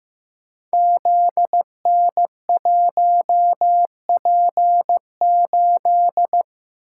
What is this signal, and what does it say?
Morse code, heard as a keyed tone, about 15 wpm, 700 Hz. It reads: ZN1P8
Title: Code Groups - 5 Letter:Numbers - 15wpm